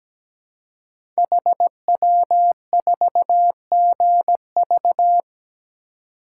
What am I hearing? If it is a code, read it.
Morse code, heard as a keyed tone, about 17 wpm, 690 Hz. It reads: HW4GV
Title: Code Groups - 5 Letter:Numbers - 17wpm